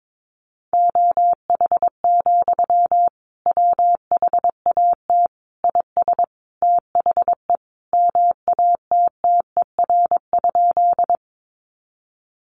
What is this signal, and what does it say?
Morse code, heard as a keyed tone, about 22 wpm, 700 Hz. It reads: OH, WHAT IS THE MATTER?
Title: Sentences from Top 700 Words 22wpm